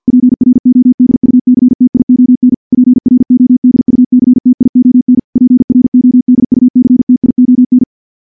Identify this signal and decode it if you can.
Morse code, heard as a keyed tone, about 48 wpm, 250 Hz. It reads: PRODUCTION PRODUCTION PRODUCTION
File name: morse / morse-training-50WPM